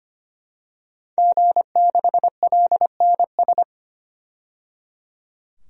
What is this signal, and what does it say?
Morse code, heard as a keyed tone, about 25 wpm, 700 Hz. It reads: G6LNS
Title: DXCC-100 Call Signs - 5 Characters - 25wpm